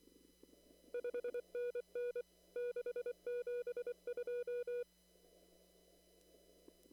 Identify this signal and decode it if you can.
Morse code, heard as a keyed tone, about 24 wpm, 490 Hz. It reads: 5NN 672
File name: CW / kt5j wpx cw-op-k5pi